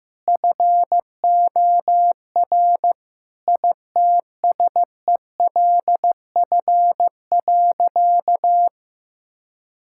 Morse code, heard as a keyed tone, about 15 wpm, 700 Hz. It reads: FOR ITSELF.
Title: Sentences from Top 1000 Words - Review Part 5 - 15wpm